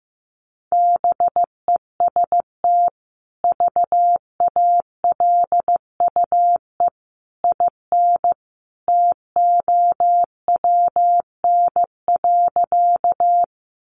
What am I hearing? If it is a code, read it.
Morse code, heard as a keyed tone, about 15 wpm, 700 Hz. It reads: BEST VALUE IN TOWN.
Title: Sentences from Top 600 Words 15wpm